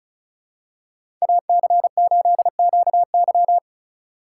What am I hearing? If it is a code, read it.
Morse code, heard as a keyed tone, about 34 wpm, 700 Hz. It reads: AC8QY
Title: USA Call Signs - 5 Characters - 2x Character Spacing - 35wpm